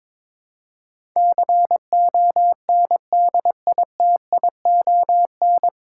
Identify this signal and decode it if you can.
Morse code, heard as a keyed tone, about 22 wpm, 700 Hz. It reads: CONDITION
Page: Morse Code Ninja practice 22wpm